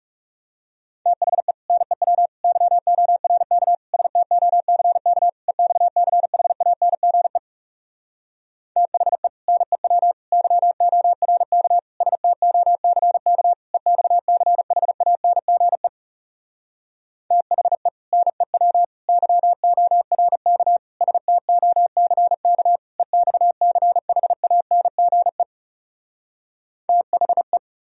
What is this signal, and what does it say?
Morse code, heard as a keyed tone, about 40 wpm, 700 Hz. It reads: THE NEW YORK STOCK EXCHANGE THE NEW YORK STOCK EXCHANGE THE NEW YORK STOCK EXCHANGE THE